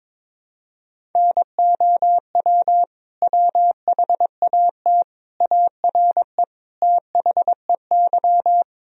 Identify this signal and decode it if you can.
Morse code, heard as a keyed tone, about 22 wpm, 700 Hz. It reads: NOW WHAT ARE THEY
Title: Sentences from Top 1000 Words - Review Part 1 - 22wpm